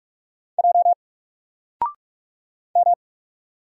Morse code, heard as a keyed tone, about 44 wpm, 700 Hz. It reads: J M
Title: Single Character (Letter:Number:Punctuation:Prosign) 45wpm